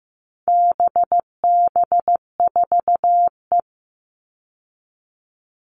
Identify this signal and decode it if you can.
Morse code, heard as a keyed tone, about 15 wpm, 700 Hz. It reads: BB4E
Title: All Cut Numbers - 4-Digit 15wpm